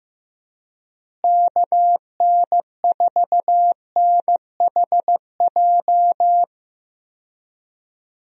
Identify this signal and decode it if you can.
Morse code, heard as a keyed tone, about 15 wpm, 700 Hz. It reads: KN4NHJ
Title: USA Call Signs - 6 Characters - Repeated 2x - 15wpm